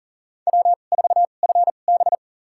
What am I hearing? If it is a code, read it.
Morse code, heard as a keyed tone, about 37 wpm, spaced as for 28 wpm, 700 Hz. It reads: W4FB
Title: USA Call Signs - 4 Characters - 2x Character Spacing - 40wpm